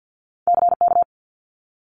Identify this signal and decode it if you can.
Morse code, heard as a keyed tone, about 50 wpm, 700 Hz. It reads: CK
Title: Multi-letter Phonemes Quick Fire 50wpm